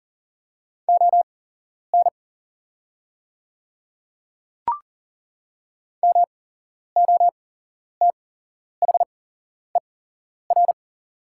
Morse code, heard as a keyed tone, about 40 wpm, spaced as for 11 wpm, 690 Hz. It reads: ON MOTHER
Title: 8x Character Spacing - Top 300 Words - 40wpm